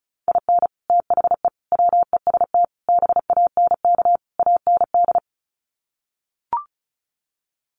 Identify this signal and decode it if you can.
Morse code, heard as a keyed tone, about 34 wpm, 700 Hz. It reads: IN THE WEST BANK AND
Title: Sets of 5 Words Encore2 35wpm